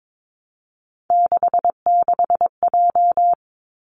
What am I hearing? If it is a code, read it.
Morse code, heard as a keyed tone, about 22 wpm, 700 Hz. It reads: 66J
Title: Three Challenging Letter-Number Combinations 22wpm